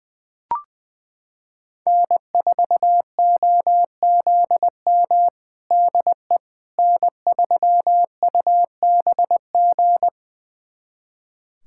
Morse code, heard as a keyed tone, about 21 wpm, 700 Hz. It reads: N4OZM DE N3UBG
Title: USA Callsign DE Callsign - 5 Characters - 20wpm